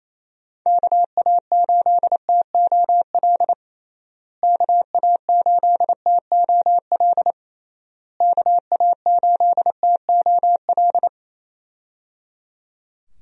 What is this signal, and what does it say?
Morse code, heard as a keyed tone, about 27 wpm, 700 Hz. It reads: KA8TOL KA8TOL KA8TOL
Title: USA Call Signs - 6 Characters - Repeated 3x - 28wpm